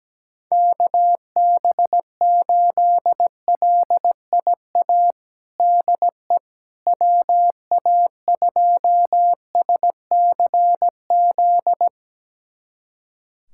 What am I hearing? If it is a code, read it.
Morse code, heard as a keyed tone, about 17 wpm, 700 Hz. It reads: KB8LIA DE WA2SCZ
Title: USA Callsign DE Callsign - 6 Characters - 17wpm